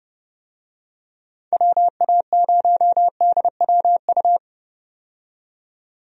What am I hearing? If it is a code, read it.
Morse code, heard as a keyed tone, about 30 wpm, 700 Hz. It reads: WA0DWU
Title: USA Call Signs - 6 Characters - 30wpm